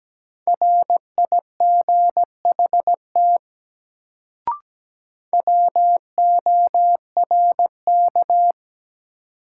Morse code, heard as a keyed tone, about 17 wpm, 700 Hz. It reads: RIGHT WORK
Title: Sets of 2 Words - New Word Drill - 17wpm